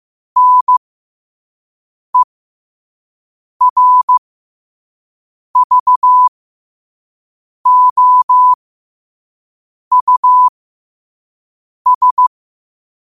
Morse code, heard as a keyed tone, about 15 wpm, spaced as for 5 wpm, 1000 Hz. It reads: NERVOUS
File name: qotdfiles / 5wpm